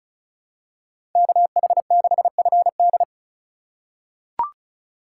Morse code, heard as a keyed tone, about 36 wpm, 700 Hz. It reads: KH6FD
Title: USA Call Signs - 5 Characters - 2x Character Spacing - 35wpm